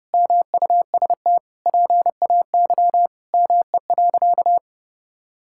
Morse code, heard as a keyed tone, about 30 wpm, 700 Hz. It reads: MUST PAY ME.
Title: Sentences from Top 1000 Words - Review Part 5 - 30wpm